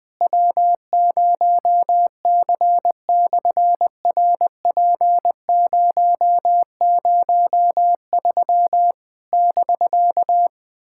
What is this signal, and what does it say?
Morse code, heard as a keyed tone, about 20 wpm, 700 Hz. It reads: W0C/RP003 <BK>
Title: SOTA QSO Practice 20wpm